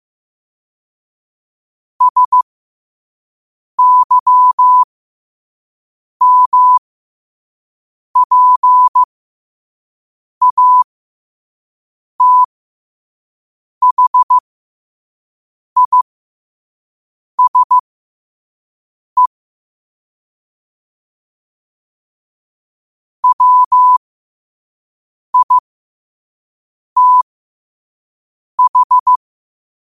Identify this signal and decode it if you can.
Morse code, heard as a keyed tone, about 15 wpm, spaced as for 5 wpm, 1000 Hz. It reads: SYMPATHISE WITH
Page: New quotes every day in morse code at 5 Words per minute.